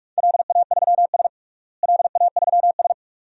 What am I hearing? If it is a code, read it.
Morse code, heard as a keyed tone, about 47 wpm, 700 Hz. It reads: LA3S LA3S
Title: DXCC-100 Call Signs - 4 Characters - Repeated 2x - 45wpm